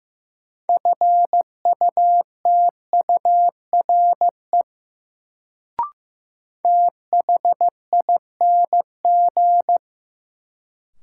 Morse code, heard as a keyed tone, about 15 wpm, 700 Hz. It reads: FUTURE THING
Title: Sets of 3 Words - New Word Drill - 15wpm